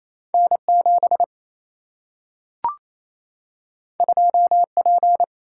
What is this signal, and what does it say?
Morse code, heard as a keyed tone, about 28 wpm, 700 Hz. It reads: N 7 2 P
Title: Single-Letter Single-Number 28wpm